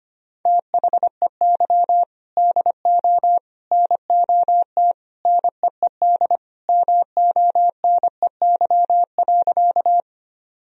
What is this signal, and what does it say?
Morse code, heard as a keyed tone, about 25 wpm, 700 Hz. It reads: THEY DO NOT NEED MONEY.
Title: Sentences from Top 1000 Words - Review Part 1 - 25wpm